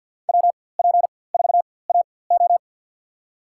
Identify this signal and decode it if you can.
Morse code, heard as a keyed tone, about 52 wpm, spaced as for 23 wpm, 700 Hz. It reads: WP4AO
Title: USA Call Signs - 5 Characters - 4x Character Spacing - 50wpm